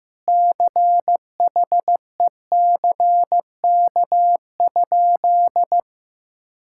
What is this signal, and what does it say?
Morse code, heard as a keyed tone, about 15 wpm, 700 Hz. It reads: CHECK?
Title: Sentences from Top 1000 Words - Review Part 6 - 15wpm